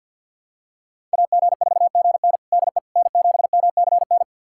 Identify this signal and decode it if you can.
Morse code, heard as a keyed tone, about 48 wpm, 700 Hz. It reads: AG4GN DE N7MCN
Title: USA Callsign DE Callsign - 5 Characters - 50wpm